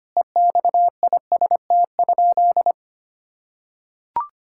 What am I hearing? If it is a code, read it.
Morse code, heard as a keyed tone, about 25 wpm, 700 Hz. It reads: EXIST?
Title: Sentences from Top 1000 Words - Review Part 5 - 25wpm